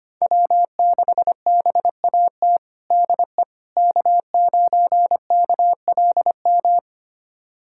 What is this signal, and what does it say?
Morse code, heard as a keyed tone, about 25 wpm, 700 Hz. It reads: W6BAT DE K9KLM
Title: USA Callsign DE Callsign - 5 Characters - 25wpm